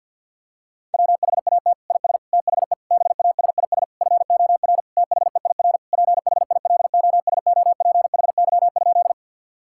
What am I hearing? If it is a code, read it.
Morse code, heard as a keyed tone, about 52 wpm, 700 Hz. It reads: WHAT IS THE BASIS FOR THEIR PHILOSOPHY?
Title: Sentences from Top 1000 Words - Review Part 6 - 50wpm